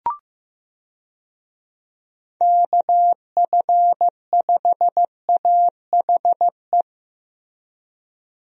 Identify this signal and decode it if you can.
Morse code, heard as a keyed tone, about 15 wpm, 700 Hz. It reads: KF5AHE
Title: USA Call Signs - 6 Characters - Repeated 2x - 15wpm